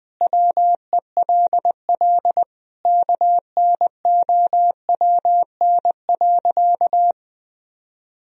Morse code, heard as a keyed tone, about 21 wpm, 700 Hz. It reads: WELL KNOWN.
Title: Sentences from Top 1000 Words - Review Part 3 - 20wpm